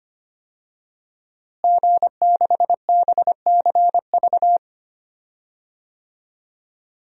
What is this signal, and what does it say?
Morse code, heard as a keyed tone, about 25 wpm, 700 Hz. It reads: G6BCV
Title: DXCC-100 Call Signs - 5 Characters - Repeated 3x - 25wpm